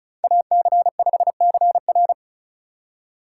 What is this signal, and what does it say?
Morse code, heard as a keyed tone, about 34 wpm, 700 Hz. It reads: AC5CR
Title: USA Call Signs - 5 Characters - 3x Character Spacing - 35wpm